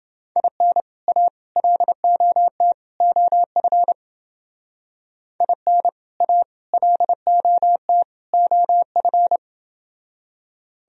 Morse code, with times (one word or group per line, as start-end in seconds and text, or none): 0.4-0.8 IN
1.1-1.3 A
1.6-2.7 LOT
3.0-3.9 OF
5.4-5.9 IN
6.2-6.4 A
6.7-8.0 LOT
8.3-9.4 OF